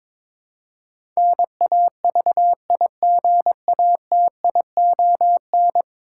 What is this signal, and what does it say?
Morse code, heard as a keyed tone, about 22 wpm, 700 Hz. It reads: NAVIGATION
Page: morse-code-ninja-practice-22wpm